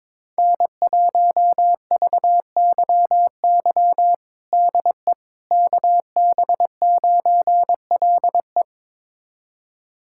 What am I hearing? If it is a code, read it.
Morse code, heard as a keyed tone, about 22 wpm, 700 Hz. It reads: N1VYY DE KB9LE
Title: USA Callsign DE Callsign - 5 Characters - 22wpm